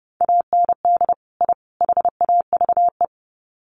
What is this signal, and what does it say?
Morse code, heard as a keyed tone, about 30 wpm, 700 Hz. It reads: AND I HAVE
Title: Sets of 3 Words 30wpm